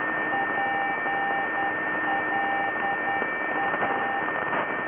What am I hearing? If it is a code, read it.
Morse code, heard as a keyed tone, about 10 wpm, 790 Hz, with radio noise. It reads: PLA
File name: NDB+Pola+PLA